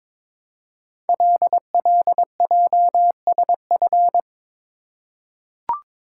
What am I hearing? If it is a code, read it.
Morse code, heard as a keyed tone, about 22 wpm, 700 Hz. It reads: LLJSF
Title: Code Groups - 5 Letters - 22wpm